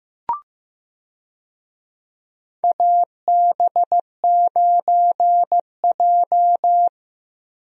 Morse code, heard as a keyed tone, about 15 wpm, 700 Hz. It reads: AB9J
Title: USA Call Signs - 4 Characters - Repeated 2x - 15wpm